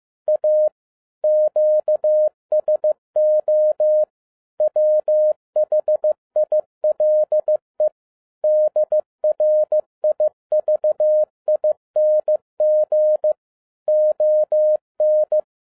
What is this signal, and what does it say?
Morse code, heard as a keyed tone, about 15 wpm, 610 Hz. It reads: A QSO WHILE DRIVING ON
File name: 2012-01-mp3 / n07-ARA JR0DIL-15wpm